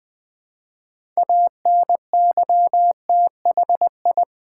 Morse code, heard as a keyed tone, about 20 wpm, 700 Hz. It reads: ANYTHI
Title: Build A Word - 2 Letters At A Time - 8-Letter Words - 20wpm